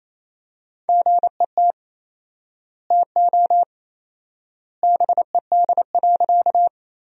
Morse code, heard as a keyed tone, about 28 wpm, 700 Hz. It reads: GET TO BED.
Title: 4x Word-Spacing - Sentences from Top 500 Words - 28wpm